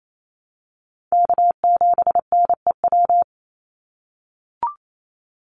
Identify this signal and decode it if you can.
Morse code, heard as a keyed tone, about 28 wpm, 700 Hz. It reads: K7NEW
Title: NA SOTA Chaser Callsigns 28wpm